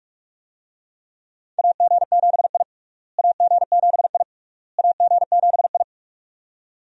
Morse code, heard as a keyed tone, about 46 wpm, 700 Hz. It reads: AG7I AG7I AG7I
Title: USA Call Signs - 4 Characters - Repeated 3x - 45wpm